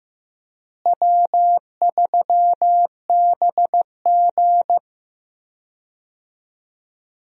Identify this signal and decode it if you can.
Morse code, heard as a keyed tone, about 15 wpm, 700 Hz. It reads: W3BG
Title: USA Call Signs - 4 Characters - Repeated 2x - 15wpm